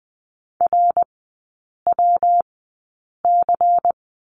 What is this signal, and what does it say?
Morse code, heard as a keyed tone, about 20 wpm, spaced as for 8 wpm, 700 Hz. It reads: RWC
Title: 121 Lesson 20e - Number 3 - Three Characters 20wpm